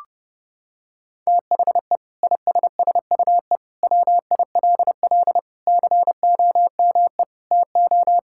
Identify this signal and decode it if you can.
Morse code, heard as a keyed tone, about 30 wpm, 700 Hz. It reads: THE ISSUE WILL COME TO
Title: Sentences from Top 1000 Words - Review Part 6 - 30wpm